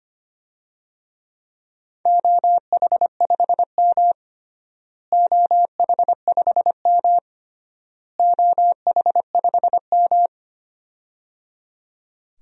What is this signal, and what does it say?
Morse code, heard as a keyed tone, about 25 wpm, 700 Hz. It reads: OH5M OH5M OH5M
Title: DXCC-100 Call Signs - 4 Characters - Repeated 3x - 25wpm